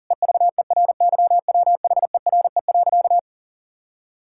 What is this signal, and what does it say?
Morse code, heard as a keyed tone, about 40 wpm, 700 Hz. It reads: EVERYWHERE.
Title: Sentences from Top 1600 Words 40wpm